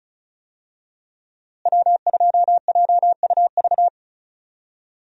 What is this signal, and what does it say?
Morse code, heard as a keyed tone, about 34 wpm, 700 Hz. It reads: W2JUV
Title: USA Call Signs - 5 Characters - Repeated 2x - 35wpm